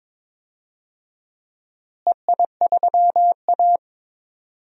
Morse code, heard as a keyed tone, about 22 wpm, 700 Hz. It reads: EI3A
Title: DXCC-100 Call Signs - 4 Characters - Repeated 3x - 22wpm